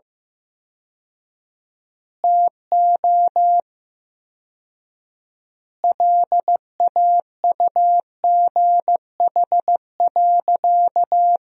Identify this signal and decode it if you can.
Morse code, heard as a keyed tone, about 15 wpm, 700 Hz. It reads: TO LAUGH.
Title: 4x Word-Spacing - Sentences from Top 500 Words - 15wpm